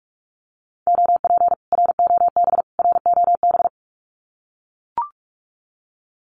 Morse code, heard as a keyed tone, about 44 wpm, 700 Hz. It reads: OP ROB ROB
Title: Common QSO Phrases Encore 45wpm